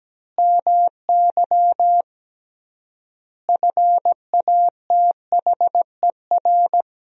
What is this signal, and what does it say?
Morse code, heard as a keyed tone, about 17 wpm, 700 Hz. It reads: MY FATHER
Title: 3x Word-Spacing - Sentences from Top 500 Words - 17wpm